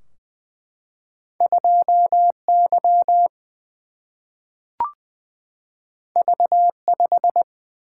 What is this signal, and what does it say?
Morse code, heard as a keyed tone, about 20 wpm, 700 Hz. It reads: 2Y V5
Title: Single-Letter Single-Number 20wpm